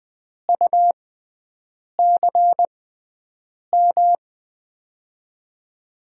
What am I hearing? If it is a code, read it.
Morse code, heard as a keyed tone, about 20 wpm, spaced as for 7 wpm, 700 Hz. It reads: UCM
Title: USA Call Signs - 6x Character Spacing - 20wpm